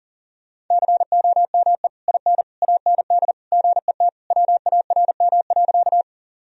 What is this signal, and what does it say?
Morse code, heard as a keyed tone, about 40 wpm, 700 Hz. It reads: COME IN AND GET WARM.
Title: Sentences from Top 1000 Words - Review Part 2 - 40wpm